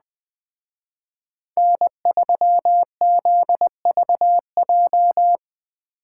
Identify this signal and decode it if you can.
Morse code, heard as a keyed tone, about 20 wpm, 700 Hz. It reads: N3ZVJ
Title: Speed Racing - USA 1x3 Call Signs 17wpm